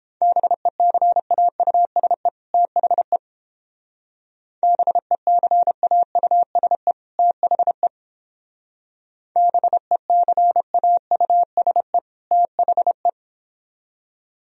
Morse code, heard as a keyed tone, about 31 wpm, 700 Hz. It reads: BECAUSE THE BECAUSE THE BECAUSE THE